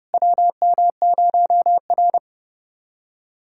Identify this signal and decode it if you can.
Morse code, heard as a keyed tone, about 31 wpm, 700 Hz. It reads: WM0R
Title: USA Callsign DE Callsign - 4 Characters - 30wpm